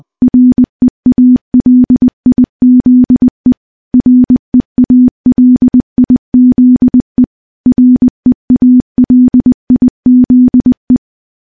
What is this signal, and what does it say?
Morse code, heard as a keyed tone, about 21 wpm, 250 Hz. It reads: REALIZE REALIZE REALIZE
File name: morse-training-20WPM